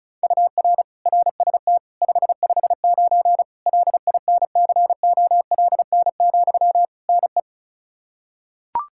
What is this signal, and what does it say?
Morse code, heard as a keyed tone, about 34 wpm, 700 Hz. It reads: UR RST 559 LINCOLN, NE
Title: Practice QSO - Exchange 1 - Beginning - 35wpm